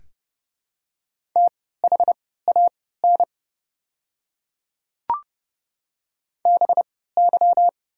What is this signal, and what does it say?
Morse code, heard as a keyed tone, about 31 wpm, spaced as for 17 wpm, 700 Hz. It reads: THAN BY
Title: 3x Character Spacing - Top 200 Words - 30wpm